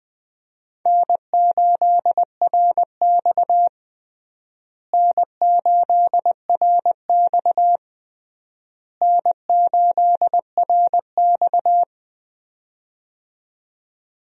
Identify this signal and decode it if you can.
Morse code, heard as a keyed tone, about 21 wpm, 700 Hz. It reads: N8RX N8RX N8RX
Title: USA Call Signs - 4 Characters - Repeated 3x - 20wpm